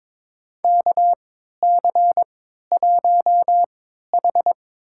Morse code, heard as a keyed tone, about 22 wpm, spaced as for 13 wpm, 700 Hz. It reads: KC1H
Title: USA Call Signs - 3x Character Spacing - 22wpm